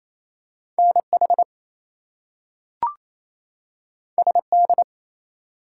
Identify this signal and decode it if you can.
Morse code, heard as a keyed tone, about 28 wpm, 700 Hz. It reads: NH SD
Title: US States and Canadian Provinces 28wpm